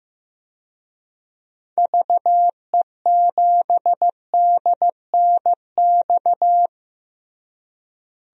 Morse code, heard as a keyed tone, about 16 wpm, 700 Hz. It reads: VE7DNX
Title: DXCC-100 Call Signs - 6 Characters - Repeated 2x - 15wpm